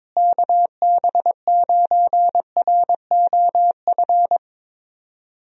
Morse code, heard as a keyed tone, about 22 wpm, 700 Hz. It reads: KB9ROF
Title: USA Call Signs - 6 Characters - Repeated 3x - 22wpm